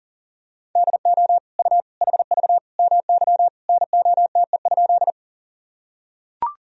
Morse code, heard as a keyed tone, about 40 wpm, 700 Hz. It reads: DO U HV MY NOTE?
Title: Sentences from Top 500 Words and Abbreviations 40wpm